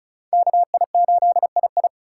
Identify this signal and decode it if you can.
Morse code, heard as a keyed tone, about 36 wpm, 700 Hz. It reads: KI8II
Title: USA Call Signs - 5 Characters - 3x Character Spacing - 35wpm